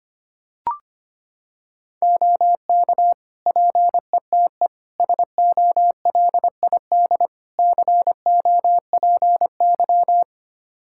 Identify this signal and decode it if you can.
Morse code, heard as a keyed tone, about 25 wpm, 700 Hz. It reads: OK PETE SOLID COPY
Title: Practice QSO - Exchange 3 - More Info - 25wpm